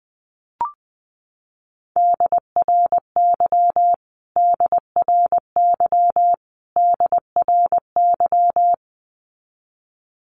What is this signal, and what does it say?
Morse code, heard as a keyed tone, about 20 wpm, 700 Hz. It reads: DRY DRY DRY
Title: Top 500 Words Repeated 20wpm